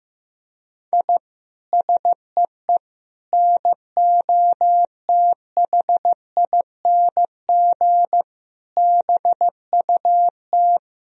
Morse code, heard as a keyed tone, about 15 wpm, 700 Hz. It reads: I SEE NOTHING BUT
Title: Sentences from Top 1000 Words - Review Part 1 - 15wpm